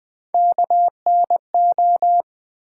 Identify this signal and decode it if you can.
Morse code, heard as a keyed tone, about 20 wpm, 700 Hz. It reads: KNO
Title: Build A Word - 1 Letter At A Time - 3 to 5-Letter Words - 20wpm